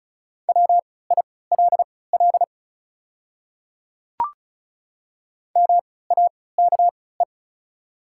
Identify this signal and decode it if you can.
Morse code, heard as a keyed tone, about 36 wpm, spaced as for 20 wpm, 690 Hz. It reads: WILL MAKE
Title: 3x Character Spacing - Top 100 Words - 35wpm